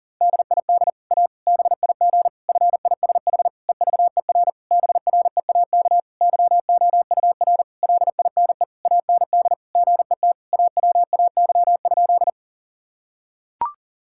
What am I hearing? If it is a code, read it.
Morse code, heard as a keyed tone, about 40 wpm, 700 Hz. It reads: DID A BIG FISH EVER BREAK YOUR LINE AND GET AWAY?
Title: Sentences from Top 1000 Words - Review Part 6 - 40wpm